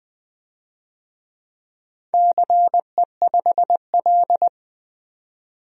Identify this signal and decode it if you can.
Morse code, heard as a keyed tone, about 20 wpm, 700 Hz. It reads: CE5L
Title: DXCC-100 Call Signs - 4 Characters - 20wpm